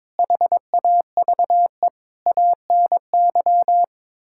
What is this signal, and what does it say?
Morse code, heard as a keyed tone, about 22 wpm, 700 Hz. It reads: HAVE ANY
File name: practice / 22wpm / Sets of 7 Words - 22wpm